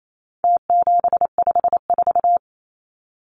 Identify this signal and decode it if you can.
Morse code, heard as a keyed tone, about 28 wpm, 700 Hz. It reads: T754
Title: Numbers - Contest Style - 4-Digit 28wpm